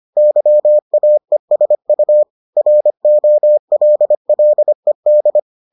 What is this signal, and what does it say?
Morse code, heard as a keyed tone, about 25 wpm, 600 Hz. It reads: YAESU ROLLED